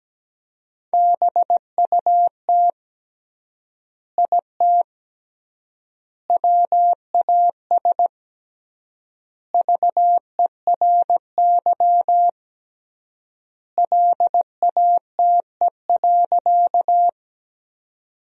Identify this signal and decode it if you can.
Morse code, heard as a keyed tone, about 17 wpm, 700 Hz. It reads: BUT IT WAS VERY LATE.
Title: 3x Word-Spacing - Sentences from Top 300 Words - 17wpm